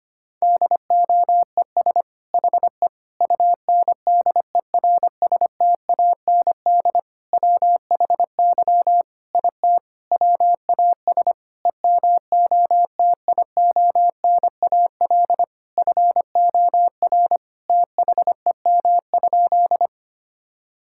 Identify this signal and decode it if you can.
Morse code, heard as a keyed tone, about 25 wpm, 700 Hz. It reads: DOES HE UNDERSTAND WHY IT WAS EMOTIONAL FOR THEM?
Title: Sentences from Top 1000 Words - Review Part 5 - 25wpm